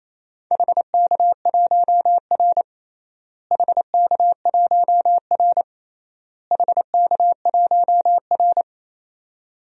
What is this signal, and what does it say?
Morse code, heard as a keyed tone, about 28 wpm, 700 Hz. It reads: HK1R HK1R HK1R
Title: DXCC-100 Call Signs - 4 Characters - Repeated 3x - 28wpm